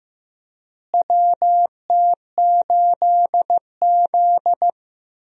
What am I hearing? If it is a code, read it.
Morse code, heard as a keyed tone, about 16 wpm, 700 Hz. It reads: WT8Z
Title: USA Call Signs - 4 Characters - Repeated 3x - 15wpm